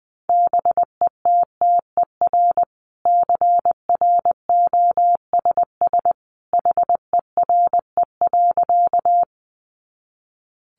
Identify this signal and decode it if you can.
Morse code, heard as a keyed tone, about 20 wpm, 700 Hz. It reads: BETTER CROSS HERE.
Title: Sentences from Top 400 Words 20wpm